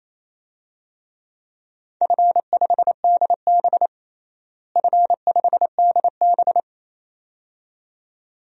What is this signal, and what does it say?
Morse code, heard as a keyed tone, about 28 wpm, 700 Hz. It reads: F5DB F5DB
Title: DXCC-100 Call Signs - 4 Characters - Repeated 2x - 28wpm